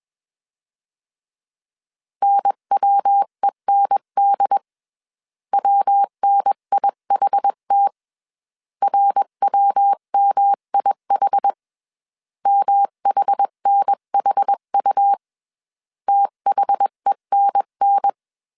Morse code, heard as a keyed tone, about 21 wpm, 800 Hz. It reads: NWEND WNIHT RWMIH MHNHU THENN
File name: CW / DET / 1 3 1500C S B 20wpm 1754